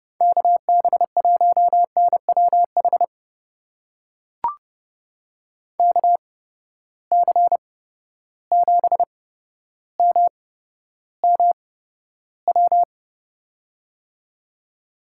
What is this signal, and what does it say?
Morse code, heard as a keyed tone, about 30 wpm, spaced as for 13 wpm, 700 Hz. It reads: KB1NWH KC7MMW
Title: USA Call Signs - 6 Characters - 8x Character Spacing - 30wpm